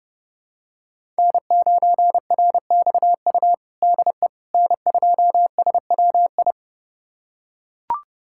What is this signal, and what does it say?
Morse code, heard as a keyed tone, about 29 wpm, 700 Hz. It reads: N9RXU DE N2SWI
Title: USA Callsign DE Callsign - 5 Characters - 30wpm